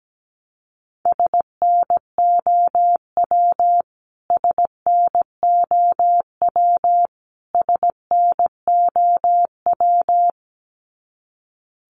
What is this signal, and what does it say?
Morse code, heard as a keyed tone, about 17 wpm, 700 Hz. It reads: SNOW SNOW SNOW
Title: Top 500 Words Repeated 17wpm